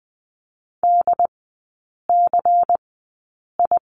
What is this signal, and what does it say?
Morse code, heard as a keyed tone, about 20 wpm, spaced as for 8 wpm, 700 Hz. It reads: DCI
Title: 093 Lesson 16e - Letter c - Three Characters 20wpm